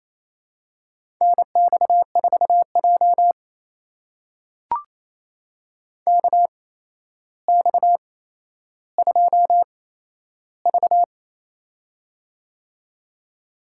Morse code, heard as a keyed tone, about 28 wpm, spaced as for 12 wpm, 700 Hz. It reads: NX4J KX2V
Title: USA Call Signs - 4 Characters - 8x Character Spacing - 28wpm